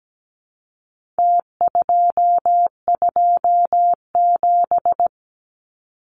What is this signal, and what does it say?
Morse code, heard as a keyed tone, about 17 wpm, 700 Hz. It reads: T227
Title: Numbers - Contest Style - 4-Digit 17wpm